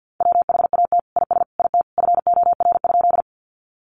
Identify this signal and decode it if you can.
Morse code, heard as a keyed tone, about 50 wpm, 700 Hz. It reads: WHAT IS IT FOR?
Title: Sentences from Top 100 Words 50wpm